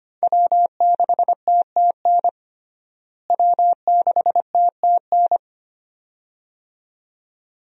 Morse code, heard as a keyed tone, about 25 wpm, 700 Hz. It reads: W6TTN W6TTN
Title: USA Call Signs - 5 Characters - Repeated 2x - 25wpm